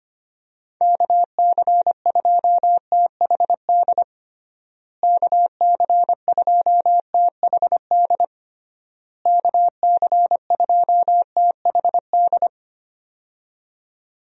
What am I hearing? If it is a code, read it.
Morse code, heard as a keyed tone, about 25 wpm, 700 Hz. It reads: KC2THD KC2THD KC2THD
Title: USA Call Signs - 6 Characters - Repeated 3x - 25wpm